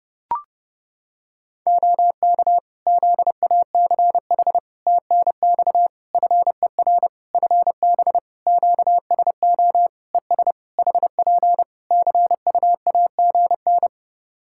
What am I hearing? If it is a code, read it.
Morse code, heard as a keyed tone, about 32 wpm, 700 Hz. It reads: OK ZACH TNX FER FB QSO ES HP CUAGN
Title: K1USN SST QSO Practice 30wpm